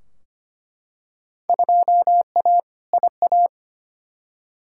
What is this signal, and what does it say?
Morse code, heard as a keyed tone, about 25 wpm, 700 Hz. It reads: 2A IA
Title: Field Day Exchanges - 25wpm